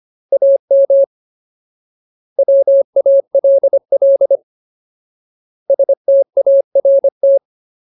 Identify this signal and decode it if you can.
Morse code, heard as a keyed tone, about 25 wpm, 550 Hz. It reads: AM WALL START
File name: Ultimate Word List / 25WPM Ultimate